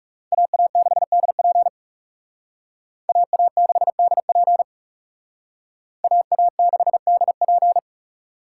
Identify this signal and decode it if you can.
Morse code, heard as a keyed tone, about 35 wpm, 700 Hz. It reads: AA6DP AA6DP AA6DP
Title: Speed Racing - USA 2x2 Call Signs 30wpm